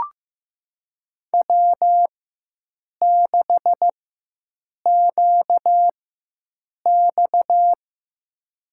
Morse code, heard as a keyed tone, about 15 wpm, spaced as for 7 wpm, 700 Hz. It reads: W6QX
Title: USA Call Signs - 4 Characters - 4x Character Spacing - 15wpm